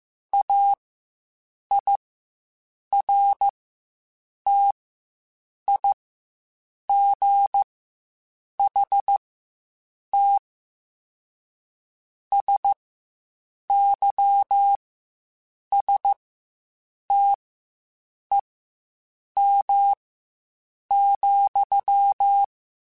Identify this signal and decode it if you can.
Morse code, heard as a keyed tone, about 15 wpm, spaced as for 7 wpm, 800 Hz. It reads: AIRTIGHT SYSTEM,
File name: Quotes Spaced / Quotes Spaced-mp3s / Airtight sp @10